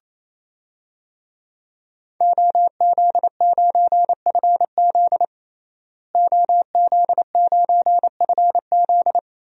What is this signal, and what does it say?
Morse code, heard as a keyed tone, about 28 wpm, 700 Hz. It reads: OZ9FZ OZ9FZ
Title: DXCC-100 Call Signs - 5 Characters - Repeated 3x - 28wpm